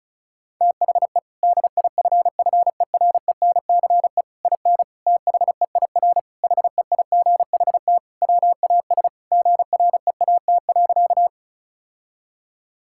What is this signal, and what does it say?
Morse code, heard as a keyed tone, about 36 wpm, 700 Hz. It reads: THE DIFFERENCE IN THEIR HEIGHT WAS GREAT.
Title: Sentences from Top 1600 Words 35wpm